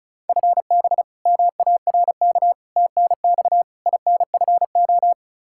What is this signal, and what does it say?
Morse code, heard as a keyed tone, about 36 wpm, 700 Hz. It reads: FB MARK TNX INFO
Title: K1USN SST QSO Practice 35wpm